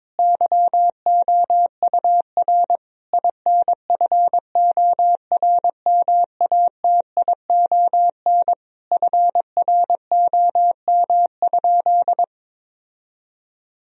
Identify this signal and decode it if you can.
Morse code, heard as a keyed tone, about 22 wpm, 700 Hz. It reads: YOUR INFORMATION FROM?
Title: Sentences from Top 1000 Words - Review Part 3 - 22wpm